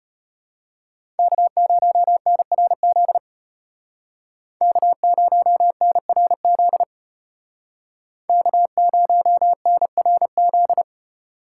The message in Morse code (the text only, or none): K0NRZ K0NRZ K0NRZ